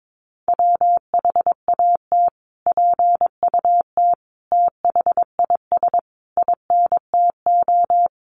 Morse code, heard as a keyed tone, about 22 wpm, 700 Hz. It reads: WHAT PUT THIS INTO
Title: Sentences from Top 600 Words 22wpm